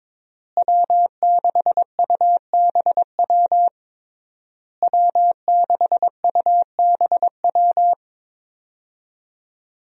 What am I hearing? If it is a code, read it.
Morse code, heard as a keyed tone, about 22 wpm, 700 Hz. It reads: W6UBW W6UBW
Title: USA Call Signs - 5 Characters - Repeated 3x - 22wpm